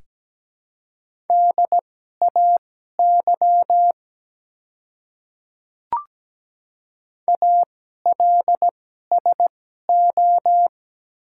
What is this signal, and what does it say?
Morse code, heard as a keyed tone, about 17 wpm, spaced as for 12 wpm, 700 Hz. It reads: DAY ALSO
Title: 2x Character Spacing - Top 200 Words - 17wpm